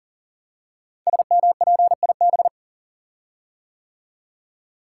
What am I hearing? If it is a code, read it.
Morse code, heard as a keyed tone, about 42 wpm, 700 Hz. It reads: SMPIB
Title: Code Groups - 5 Letters - 40wpm